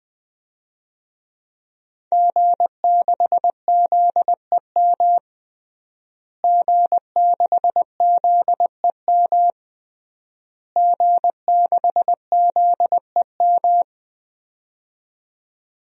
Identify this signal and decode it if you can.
Morse code, heard as a keyed tone, about 20 wpm, 700 Hz. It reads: G6ZEM G6ZEM G6ZEM
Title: DXCC-100 Call Signs - 5 Characters - Repeated 3x - 20wpm